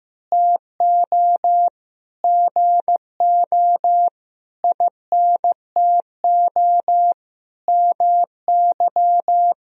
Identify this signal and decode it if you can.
Morse code, heard as a keyed tone, about 15 wpm, 700 Hz. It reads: TO GO INTO MY
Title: Sentences from Top 1000 Words - Review Part 1 - 15wpm